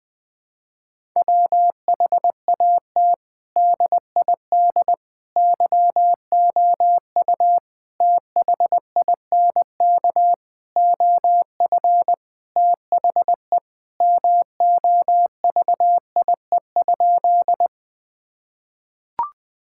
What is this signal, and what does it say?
Morse code, heard as a keyed tone, about 20 wpm, 700 Hz. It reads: WHAT DID YOU THINK OF THE MOVIE?
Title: Sentences from Top 1000 Words - Review Part 5 - 20wpm